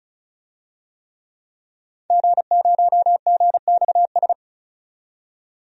Morse code, heard as a keyed tone, about 36 wpm, 700 Hz. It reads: G0GXS
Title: DXCC-100 Call Signs - 5 Characters - 35wpm